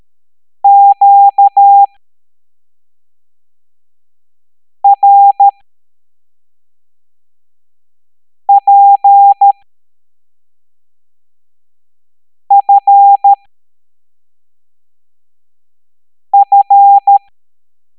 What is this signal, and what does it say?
Morse code, heard as a keyed tone, about 13 wpm, spaced as for 3 wpm, 800 Hz. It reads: QRPFF